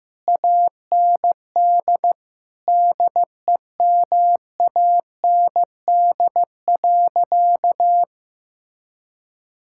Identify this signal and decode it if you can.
Morse code, heard as a keyed tone, about 15 wpm, 700 Hz. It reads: AND DEMAND.
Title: Sentences from Top 1000 Words - Review Part 5 - 15wpm